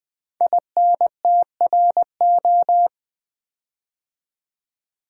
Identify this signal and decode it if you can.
Morse code, heard as a keyed tone, about 20 wpm, 700 Hz. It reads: INTRO
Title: Build A Word - 5 Letters At A Time - 12-Letter Words - 20wpm